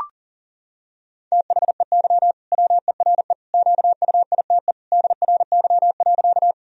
Morse code, heard as a keyed tone, about 37 wpm, 700 Hz. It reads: THEY WERE QUITE DRY.
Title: Sentences from Top 1000 Words - Review Part 3 - 40wpm